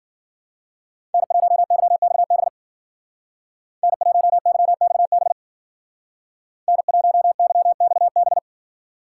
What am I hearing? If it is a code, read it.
Morse code, heard as a keyed tone, about 53 wpm, 700 Hz. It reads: N1YXB N1YXB N1YXB